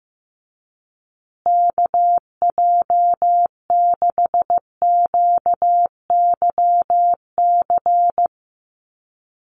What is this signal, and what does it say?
Morse code, heard as a keyed tone, about 15 wpm, 700 Hz. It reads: KJ6QYC
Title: USA Call Signs Repeated 2x 15wpm